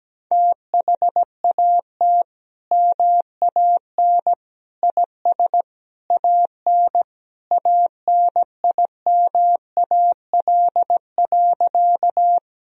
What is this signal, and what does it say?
Morse code, heard as a keyed tone, about 17 wpm, 700 Hz. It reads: THAT MAN IS AN ANIMAL.
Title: Sentences from Top 1000 Words - Review Part 1 - 17wpm